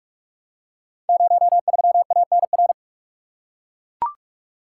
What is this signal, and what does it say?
Morse code, heard as a keyed tone, about 44 wpm, 700 Hz. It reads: 03ANR
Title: Code Groups - 5 Letter:Numbers - 45wpm